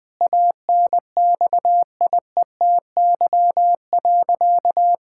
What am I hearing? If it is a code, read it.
Morse code, heard as a keyed tone, about 20 wpm, 700 Hz. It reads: ANXIETY.
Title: Sentences from Top 1600 Words 20wpm